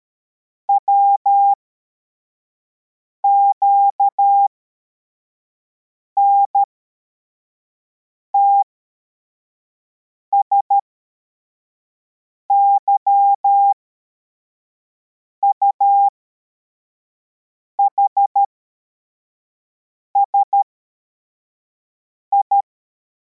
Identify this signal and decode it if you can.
Morse code, heard as a keyed tone, about 13 wpm, spaced as for 4 wpm, 800 Hz. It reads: WQNTS YUHSI
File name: formation / cw / Text81